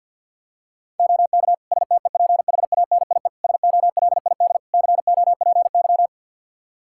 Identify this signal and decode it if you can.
Morse code, heard as a keyed tone, about 47 wpm, 700 Hz. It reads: OK STEPHANIE SOLID COPY
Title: Practice QSO - Exchange 3 - More Info - 50wpm